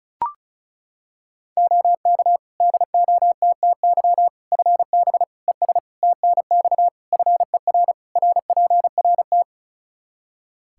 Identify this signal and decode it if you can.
Morse code, heard as a keyed tone, about 36 wpm, 700 Hz. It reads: OK DOTTY FB ES TNX FER RPRT
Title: K1USN SST QSO Practice 35wpm